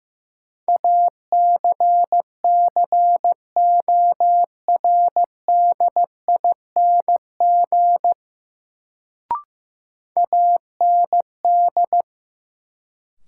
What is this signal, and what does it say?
Morse code, heard as a keyed tone, about 15 wpm, 700 Hz. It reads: ACCORDING AND
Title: Sets of 2 Words - New Word Drill - 15wpm